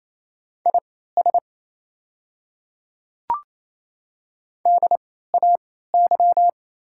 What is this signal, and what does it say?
Morse code, heard as a keyed tone, about 28 wpm, spaced as for 16 wpm, 700 Hz. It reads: IS DAY
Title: 3x Character Spacing - Top 100 Words - 28wpm